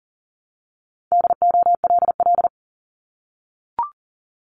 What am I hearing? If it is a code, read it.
Morse code, heard as a keyed tone, about 39 wpm, 700 Hz. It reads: DOLL
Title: Multi-letter Phonemes Repeated 3x 40wpm